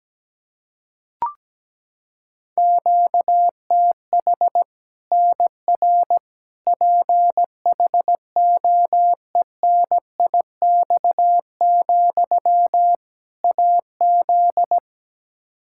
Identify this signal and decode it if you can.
Morse code, heard as a keyed tone, about 17 wpm, 700 Hz. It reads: QTH NR PHOENIX, AZ
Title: Practice QSO - Exchange 1 - Beginning - 17wpm